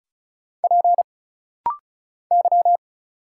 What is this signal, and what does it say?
Morse code, heard as a keyed tone, about 35 wpm, 700 Hz. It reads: P Y
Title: Single Letter-Number Alt - Rapid-Fire 35wpm